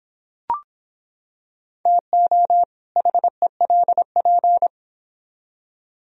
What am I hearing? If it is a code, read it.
Morse code, heard as a keyed tone, about 25 wpm, 700 Hz. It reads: TO HELP